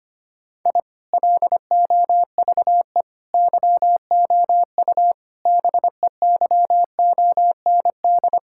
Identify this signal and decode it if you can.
Morse code, heard as a keyed tone, about 25 wpm, 700 Hz. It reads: I LOVE YOU BEYOND
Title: Sentences from Top 1000 Words - Review Part 6 - 25wpm